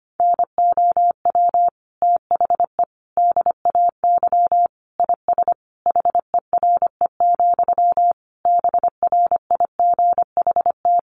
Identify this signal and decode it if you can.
Morse code, heard as a keyed tone, about 25 wpm, 700 Hz. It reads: NOW THE DAY IS HERE, BRIGHT
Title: Sentences from Top 1000 Words 25wpm